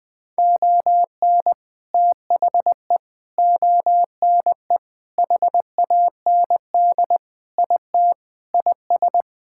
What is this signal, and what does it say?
Morse code, heard as a keyed tone, about 20 wpm, 700 Hz. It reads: ON THE ONE HAND IT IS
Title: Sentences with 8-Word Phrases - 20wpm